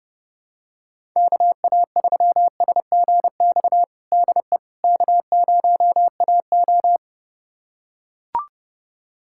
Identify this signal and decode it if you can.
Morse code, heard as a keyed tone, about 30 wpm, 700 Hz. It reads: KA3SGX DE K0AO
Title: USA Callsign DE Callsign 30wpm